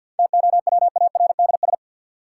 Morse code, heard as a keyed tone, about 50 wpm, 700 Hz. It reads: TOWARDS
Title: Sentences from Top 1100 Words - New Word Drill Alt - 50wpm